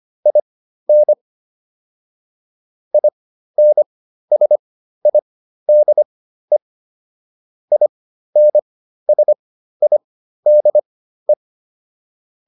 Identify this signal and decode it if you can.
Morse code, heard as a keyed tone, about 25 wpm, spaced as for 13 wpm, 600 Hz. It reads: IN INSIDE INSIDE